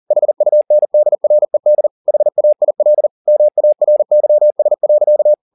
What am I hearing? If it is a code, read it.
Morse code, heard as a keyed tone, about 40 wpm, 600 Hz. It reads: HUNDRED HAIL MARYS.
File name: the old man and the sea 2 40wpm